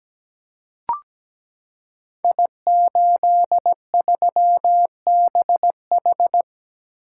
Morse code, heard as a keyed tone, about 17 wpm, 700 Hz. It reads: I83BH
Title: Code Groups - 5 Letter:Numbers - 17wpm